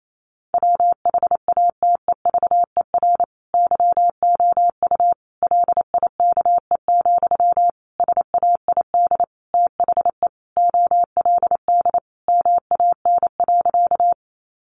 Morse code, heard as a keyed tone, about 28 wpm, 700 Hz. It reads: WHATEVER YOU LIKE, SAID THE OLD MAN.
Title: Sentences from Top 1100 Words 28wpm